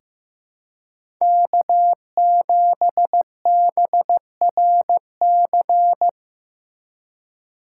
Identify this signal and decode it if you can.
Morse code, heard as a keyed tone, about 15 wpm, 700 Hz. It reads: K7BRC
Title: USA Call Signs - 5 Characters - 2x Character Spacing - 15wpm